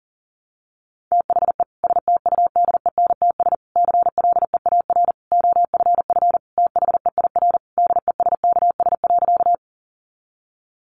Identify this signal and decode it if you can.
Morse code, heard as a keyed tone, about 40 wpm, 700 Hz. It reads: THE STUDENTS CLEAR OFF THEIR DESKS.
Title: Additional Easy Phrases 40wpm